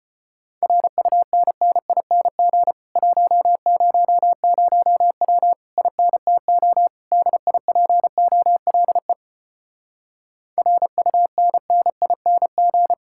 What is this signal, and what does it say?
Morse code, heard as a keyed tone, about 33 wpm, 700 Hz. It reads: RUNNING 100W INTO DIPOLE RUNNING